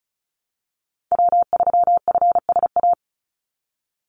Morse code, heard as a keyed tone, about 34 wpm, 700 Hz. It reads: W3FSA
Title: Call Signs Repeated 2x - 35wpm